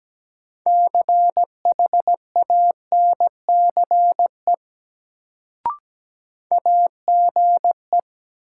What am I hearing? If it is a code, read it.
Morse code, heard as a keyed tone, about 17 wpm, 700 Hz. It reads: CHANCE AGE
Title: Sets of 3 Words - New Word Drill - 17wpm